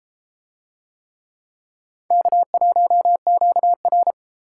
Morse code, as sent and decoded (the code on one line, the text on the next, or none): -.- .---- --.- .-.
K1QR